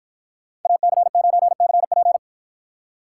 Morse code, heard as a keyed tone, about 53 wpm, 700 Hz. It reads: AC9CP